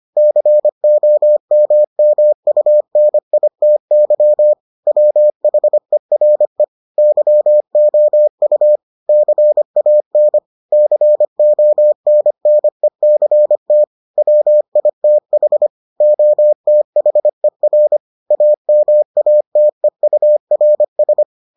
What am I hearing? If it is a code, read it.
Morse code, heard as a keyed tone, about 25 wpm, 600 Hz. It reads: COMMUNITY WHERE YOU CAN CONNECT WITH OTHER AMATEURS